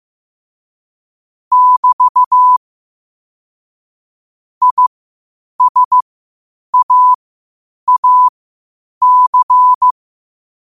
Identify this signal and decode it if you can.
Morse code, heard as a keyed tone, about 15 wpm, spaced as for 8 wpm, 1000 Hz. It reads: <BT> ISAAC
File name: qotdfiles / 8wpm